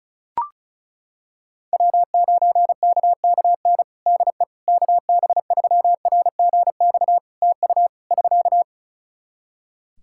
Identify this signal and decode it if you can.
Morse code, heard as a keyed tone, about 36 wpm, 700 Hz. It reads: W9KKN DE KB3RGX TU <SK>
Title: Practice QSO - Exchange 4 - Ending - 35wpm